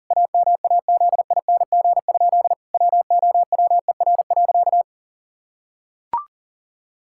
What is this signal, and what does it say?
Morse code, heard as a keyed tone, about 37 wpm, 700 Hz. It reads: AMAZING? WOWER.
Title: Pangram Challenge v2 - 40wpm